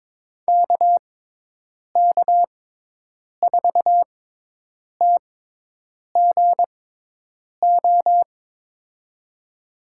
Morse code, heard as a keyed tone, about 22 wpm, spaced as for 8 wpm, 700 Hz. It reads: KK4TGO
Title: USA Call Signs - 6 Characters - 6x Character Spacing - 22wpm